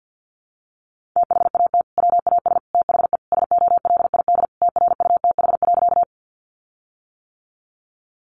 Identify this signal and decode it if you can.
Morse code, heard as a keyed tone, about 50 wpm, 700 Hz. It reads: THAT WAS THE SOLID TRUTH.
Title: Sentences from Top 1500 Words 50wpm